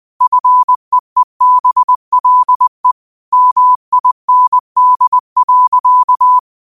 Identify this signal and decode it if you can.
Morse code, heard as a keyed tone, about 20 wpm, 1000 Hz. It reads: FEEBLE MIND.